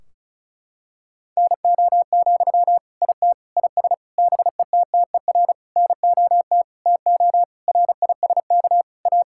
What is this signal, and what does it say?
Morse code, heard as a keyed tone, about 36 wpm, 700 Hz. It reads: NO, IT IS BETTER NOT TO RISK A
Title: Sentences from Top 1600 Words 35wpm